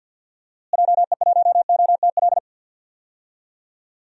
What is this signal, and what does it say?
Morse code, heard as a keyed tone, about 52 wpm, 700 Hz. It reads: JE1OTL
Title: DXCC-100 Call Signs - 6 Characters - Repeated 2x - 50wpm